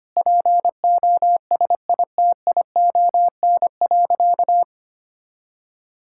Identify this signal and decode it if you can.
Morse code, heard as a keyed tone, about 25 wpm, 700 Hz. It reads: POSITION.
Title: Sentences from Top 1000 Words - Review Part 5 - 25wpm